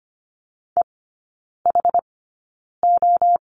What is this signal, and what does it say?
Morse code, heard as a keyed tone, about 25 wpm, spaced as for 9 wpm, 690 Hz. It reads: EHO
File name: Lesson 14e - Number 2 - Three Characters 25wpm